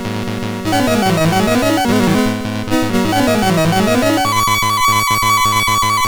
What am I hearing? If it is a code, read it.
There is no code.